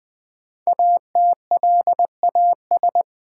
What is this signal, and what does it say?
Morse code, heard as a keyed tone, about 20 wpm, 700 Hz. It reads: ATLAS
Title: Easy Words with Hints - 20wpm